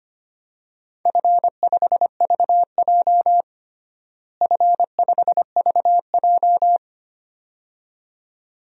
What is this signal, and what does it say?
Morse code, heard as a keyed tone, about 25 wpm, 700 Hz. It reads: F5VJ F5VJ
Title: DXCC-100 Call Signs - 4 Characters - Repeated 2x - 25wpm